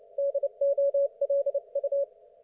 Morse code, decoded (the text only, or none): DOLU